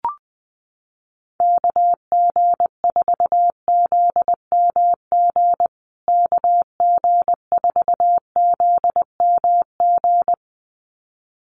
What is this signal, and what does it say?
Morse code, heard as a keyed tone, about 20 wpm, 700 Hz. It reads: KG4ZMG KG4ZMG
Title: USA Call Signs Repeated 2x 20wpm